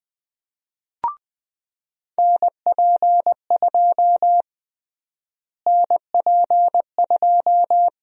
Morse code, heard as a keyed tone, about 20 wpm, 700 Hz. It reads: NP2 NP2
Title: USA Call Sign Prefixes - Repeated 2x - 20wpm